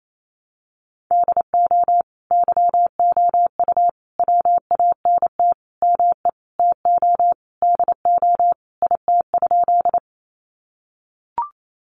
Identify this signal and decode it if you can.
Morse code, heard as a keyed tone, about 28 wpm, 700 Hz. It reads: DO YOU WANT ME TO DO IT?
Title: Sentences from Top 200 Words 28wpm